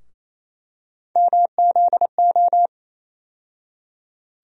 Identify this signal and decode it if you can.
Morse code, heard as a keyed tone, about 28 wpm, 700 Hz. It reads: MZO
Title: USA Call Sign Postfixes - 3 Letters 28wpm